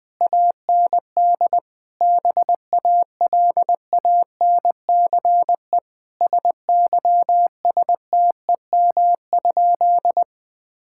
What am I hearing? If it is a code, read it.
Morse code, heard as a keyed tone, about 20 wpm, 700 Hz. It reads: AND BALANCE SYSTEM?
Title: Sentences from Top 1000 Words - Review Part 6 - 20wpm